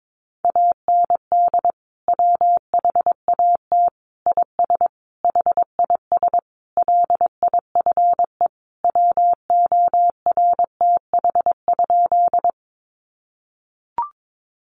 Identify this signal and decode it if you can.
Morse code, heard as a keyed tone, about 22 wpm, 700 Hz. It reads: AND WHAT IS HIS LIFE WORTH?
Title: Sentences from Top 800 Words 22wpm